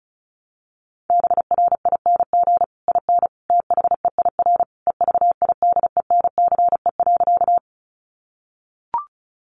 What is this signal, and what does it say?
Morse code, heard as a keyed tone, about 34 wpm, 700 Hz. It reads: BRING IN THEIR EVIDENCE.
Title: Sentences from Top 1300 Words 35wpm